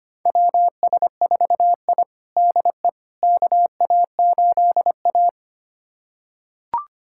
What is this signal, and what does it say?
Morse code, heard as a keyed tone, about 25 wpm, 700 Hz. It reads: WS4I DE KA8A
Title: USA Callsign DE Callsign - 4 Characters - 25wpm